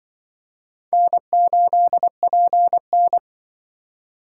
Morse code, heard as a keyed tone, about 24 wpm, 700 Hz. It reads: N8PN
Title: Speed Racing - USA Call Signs 20wpm